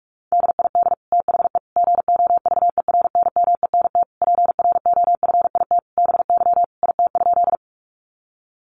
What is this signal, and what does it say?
Morse code, heard as a keyed tone, about 44 wpm, 700 Hz. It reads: DID THE GOVERNMENT PROFIT BY IT?
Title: Sentences from Top 800 Words 45wpm